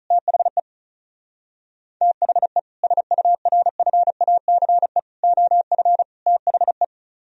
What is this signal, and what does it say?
Morse code, heard as a keyed tone, about 44 wpm, 700 Hz. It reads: THE THE SURFACE OF THE